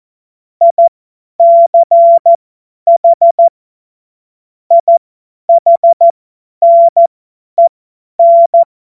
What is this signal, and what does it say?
Morse code, heard as a keyed tone, about 14 wpm, spaced as for 10 wpm, 680 Hz. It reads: ICH IHNEN
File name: Morsekurs / Witze / Witz 30 Januar 2024 50 F70